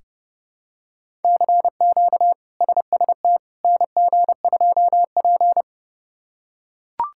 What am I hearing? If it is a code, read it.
Morse code, heard as a keyed tone, about 30 wpm, 700 Hz. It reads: CQ SST NG2P
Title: K1USN SST QSO Practice - 30wpm